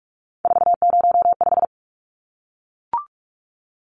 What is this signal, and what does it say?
Morse code, heard as a keyed tone, about 44 wpm, 700 Hz. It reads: 405
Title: Numbers - 3-Digit 45wpm